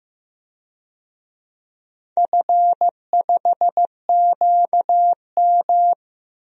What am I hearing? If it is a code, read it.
Morse code, heard as a keyed tone, about 15 wpm, 700 Hz. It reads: F5QM
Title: DXCC-100 Call Signs - 4 Characters - Repeated 3x - 15wpm